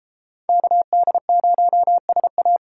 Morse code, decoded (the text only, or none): KD0SA